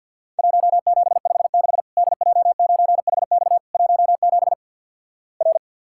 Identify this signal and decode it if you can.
Morse code, heard as a keyed tone, about 48 wpm, 700 Hz. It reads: 1756 DJ9HX 17 R
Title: Worked All Europe DX Contest - QTC Exchange - 50wpm